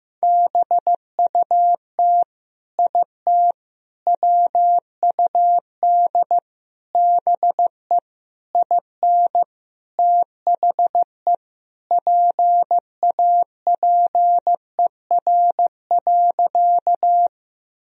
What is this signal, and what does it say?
Morse code, heard as a keyed tone, about 16 wpm, 700 Hz. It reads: BUT IT WUD BE IN THE PAPER.
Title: Sentences from Top 300 Words and Abbreviations 15wpm